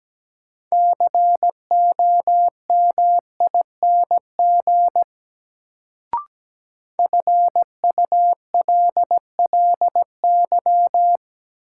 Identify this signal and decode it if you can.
Morse code, heard as a keyed tone, about 17 wpm, 700 Hz. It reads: COMING FULLY
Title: Sentences from Top 1300 Words - New Word Drill Alt - 17wpm